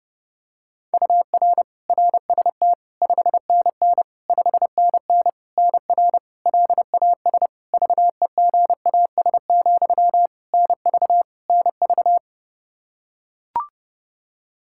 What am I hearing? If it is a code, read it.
Morse code, heard as a keyed tone, about 30 wpm, 700 Hz. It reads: UR RST 5NN 5NN NR LAS VEGAS, NV NV
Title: K1USN SST QSO Practice 30wpm